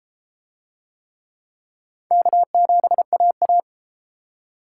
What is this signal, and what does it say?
Morse code, heard as a keyed tone, about 30 wpm, 700 Hz. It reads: K7AA